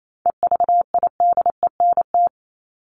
Morse code, heard as a keyed tone, about 27 wpm, 700 Hz. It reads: EVIDENT
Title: LY Suffix - Top 100 Words with Base Words 28wpm